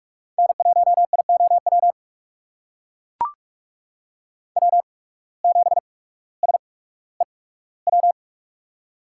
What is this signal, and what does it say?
Morse code, heard as a keyed tone, about 41 wpm, spaced as for 19 wpm, 700 Hz. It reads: N1IOW W7SEW
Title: USA Call Signs - 5 Characters - 8x Character Spacing - 45wpm